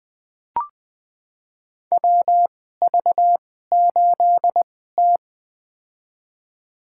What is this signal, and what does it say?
Morse code, heard as a keyed tone, about 21 wpm, spaced as for 15 wpm, 700 Hz. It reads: WV8T
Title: USA Call Signs - 4 Characters - 2x Character Spacing - 20wpm